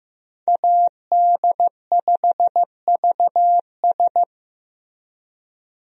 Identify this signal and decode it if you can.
Morse code, heard as a keyed tone, about 15 wpm, 700 Hz. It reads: AD5VS
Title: USA Call Signs - 5 Characters - 15wpm